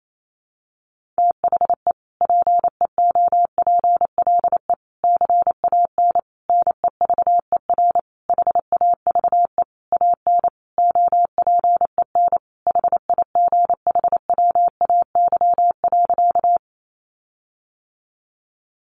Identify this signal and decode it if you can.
Morse code, heard as a keyed tone, about 28 wpm, 700 Hz. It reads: THE PEOPLE CAN NEVER HAVE AN OPEN HIGHWAY.
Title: Sentences from Top 1400 Words 28wpm